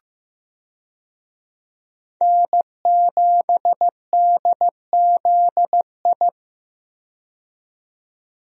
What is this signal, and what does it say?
Morse code, heard as a keyed tone, about 15 wpm, 700 Hz. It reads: N7DZI
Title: USA Call Signs - 5 Characters - 15wpm